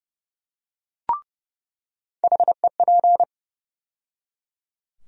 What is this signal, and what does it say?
Morse code, heard as a keyed tone, about 30 wpm, 700 Hz. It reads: HEP
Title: USA Call Sign Postfixes - 3 Letters 30wpm